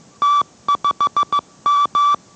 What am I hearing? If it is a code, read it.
Morse code, heard as a keyed tone, about 15 wpm, 1160 Hz. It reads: T5M